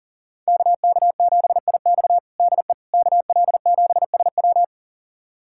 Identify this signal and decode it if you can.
Morse code, heard as a keyed tone, about 37 wpm, 700 Hz. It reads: KK7IX DE KL7SW
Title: USA Callsign DE Callsign - 5 Characters - 40wpm